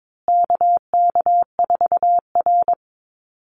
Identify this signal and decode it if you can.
Morse code, heard as a keyed tone, about 22 wpm, 700 Hz. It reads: KK4R
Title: Call Signs Repeated 2x - 22wpm